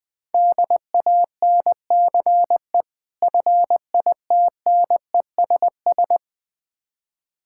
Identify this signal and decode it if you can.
Morse code, heard as a keyed tone, about 20 wpm, 700 Hz. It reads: DANCE FITNESS
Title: Difficult Words - Speed-Racing with Hints - 20wpm